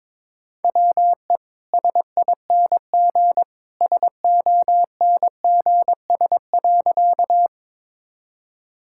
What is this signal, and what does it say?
Morse code, heard as a keyed tone, about 22 wpm, 700 Hz. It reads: WE SING SONGS.
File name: practice / 22wpm / Easy Sentences with Hints - 22wpm